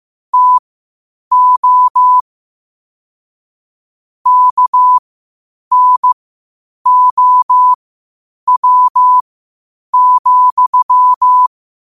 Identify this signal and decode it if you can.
Morse code, heard as a keyed tone, about 15 wpm, spaced as for 8 wpm, 1000 Hz. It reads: TO KNOW,
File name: qotdfiles / 8wpm